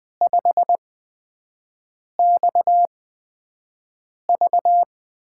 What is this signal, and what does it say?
Morse code, heard as a keyed tone, about 20 wpm, spaced as for 5 wpm, 700 Hz. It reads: 5XV
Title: USA Call Signs - 8x Character Spacing - 20wpm